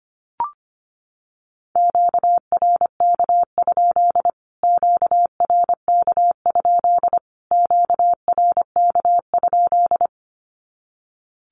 Morse code, heard as a keyed tone, about 25 wpm, 700 Hz. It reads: QRK? QRK? QRK?
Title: Q-Codes Repeated 3x 25wpm